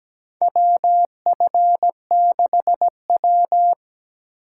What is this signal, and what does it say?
Morse code, heard as a keyed tone, about 17 wpm, 700 Hz. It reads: WF6W
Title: USA Call Signs - 4 Characters - 4x Character Spacing - 17wpm